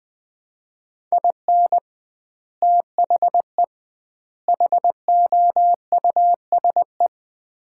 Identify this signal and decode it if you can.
Morse code, heard as a keyed tone, about 20 wpm, 700 Hz. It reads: IN THE HOUSE
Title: 2x Word-Spacing - Sets of 3 Words - 20wpm